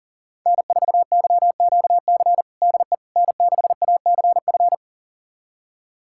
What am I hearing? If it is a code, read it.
Morse code, heard as a keyed tone, about 40 wpm, 700 Hz. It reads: N4YQC DE N6ACF
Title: USA Callsign DE Callsign - 5 Characters - 40wpm